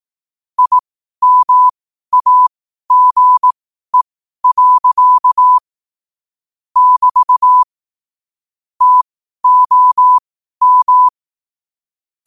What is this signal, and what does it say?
Morse code, heard as a keyed tone, about 18 wpm, spaced as for 12 wpm, 1000 Hz. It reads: IMAGE. <BT> TOM